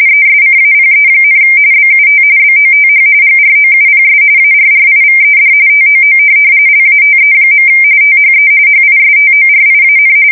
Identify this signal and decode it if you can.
RTTY, about 45.45 baud. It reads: ESCAPE, BUT BOTH GROUPS ARRIVED TOO LATE. THE HISTORIAN